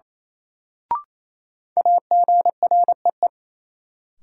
Morse code, heard as a keyed tone, about 28 wpm, 700 Hz. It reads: AGREE
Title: Sentences from Top 1000 Words - Review Part 6 - New Word Drill - 28wpm